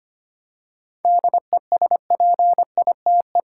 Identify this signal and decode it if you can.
Morse code, heard as a keyed tone, about 25 wpm, 700 Hz. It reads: DESPITE
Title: Seven Letter Words - 25wpm